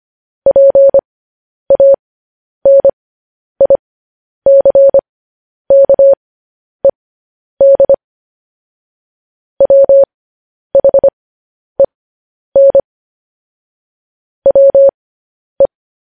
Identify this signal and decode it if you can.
Morse code, heard as a keyed tone, about 25 wpm, spaced as for 10 wpm, 550 Hz. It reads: PANICKED WHEN WE
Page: news-headlines-in-morse-code-at-10-wpm